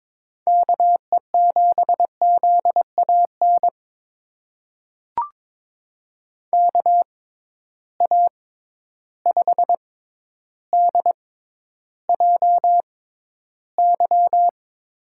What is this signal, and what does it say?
Morse code, heard as a keyed tone, about 22 wpm, spaced as for 11 wpm, 700 Hz. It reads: KE7ZAN KA5DJY
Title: USA Call Signs - 6x Character Spacing - 22wpm